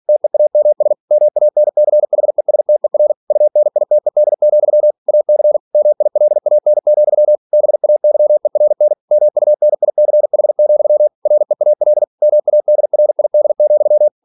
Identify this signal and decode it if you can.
Morse code, heard as a keyed tone, about 46 wpm, 600 Hz. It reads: TEAMS MANCHESTER UNITED, AC MILAN, BAYERN MUNICH, REAL MADRID,
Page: Never 47wpm